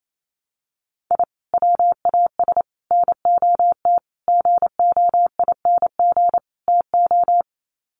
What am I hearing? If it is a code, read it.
Morse code, heard as a keyed tone, about 28 wpm, 700 Hz. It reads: I WAS NOT GOING TO
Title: Sets of 5 Words 28wpm